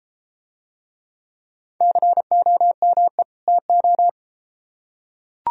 Morse code, heard as a keyed tone, about 31 wpm, 700 Hz. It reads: COME TO